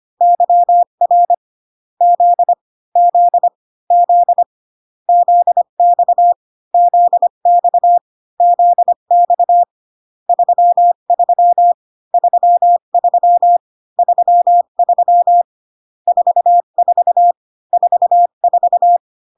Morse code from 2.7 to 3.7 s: Z